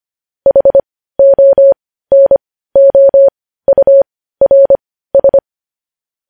Morse code, heard as a keyed tone, about 25 wpm, spaced as for 15 wpm, 550 Hz. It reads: HONOURS
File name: podcast / News headlines-1x-25x15